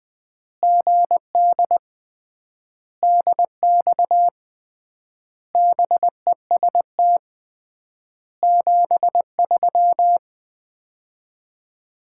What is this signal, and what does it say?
Morse code, heard as a keyed tone, about 20 wpm, 700 Hz. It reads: GD DX BEST 73
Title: Common QSO Phrases - 3x Word-Spacing - 20wpm